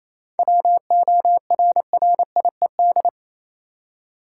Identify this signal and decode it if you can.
Morse code, heard as a keyed tone, about 28 wpm, 700 Hz. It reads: WORRIED
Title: Seven Letter Words - 28wpm